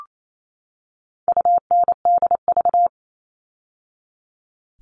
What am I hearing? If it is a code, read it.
Morse code, heard as a keyed tone, about 28 wpm, 700 Hz. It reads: UNDV
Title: All Cut Numbers - 4-Digit 28wpm